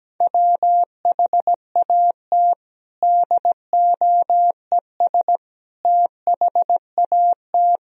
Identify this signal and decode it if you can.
Morse code, heard as a keyed tone, about 17 wpm, 700 Hz. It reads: WHAT DOES THAT
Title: Sentences from Top 1000 Words - Review Part 4 - 17wpm